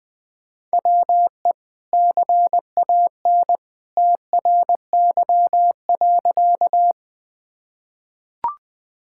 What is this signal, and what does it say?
Morse code, heard as a keyed tone, about 20 wpm, 700 Hz. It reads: WE CAN TRY.
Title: Sentences with 2-Word Phrases - 20wpm